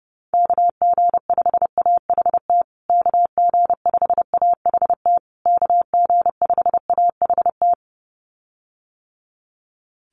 Morse code, heard as a keyed tone, about 28 wpm, 700 Hz. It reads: KG5AHT KG5AHT KG5AHT
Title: USA Call Signs Repeated 3x 30wpm